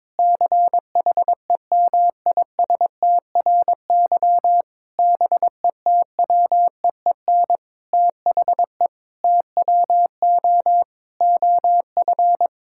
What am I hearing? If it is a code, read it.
Morse code, heard as a keyed tone, about 22 wpm, 700 Hz. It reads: CHEMISTRY BETWEEN THE TWO OF
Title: Sentences from Top 1000 Words - Review Part 6 - 22wpm